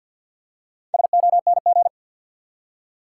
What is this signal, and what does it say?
Morse code, heard as a keyed tone, about 52 wpm, 700 Hz. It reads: SONG
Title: Four Letter Words - 50wpm